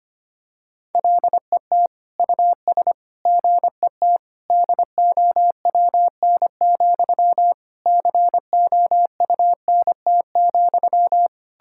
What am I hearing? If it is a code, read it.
Morse code, heard as a keyed tone, about 25 wpm, 700 Hz. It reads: LET US GET DOWN, COUNT,
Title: Sentences from Top 1600 Words 25wpm